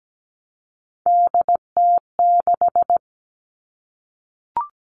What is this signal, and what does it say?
Morse code, heard as a keyed tone, about 17 wpm, 700 Hz. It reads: DT6
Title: All Cut Numbers - 3-Digit 17wpm